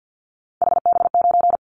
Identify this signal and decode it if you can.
Morse code, heard as a keyed tone, about 50 wpm, 700 Hz. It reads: HB9
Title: Callsign Prefixes for Top 25 DXCC - 50wpm